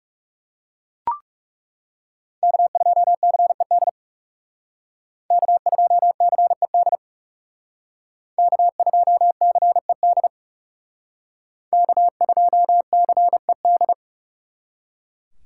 Morse code, heard as a keyed tone, about 38 wpm, 700 Hz. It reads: K2CED K2CED K2CED K2CED